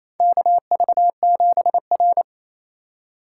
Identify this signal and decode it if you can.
Morse code, heard as a keyed tone, about 28 wpm, 700 Hz. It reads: KV7R
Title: USA Call Signs - 4 Characters - 4x Character Spacing - 28wpm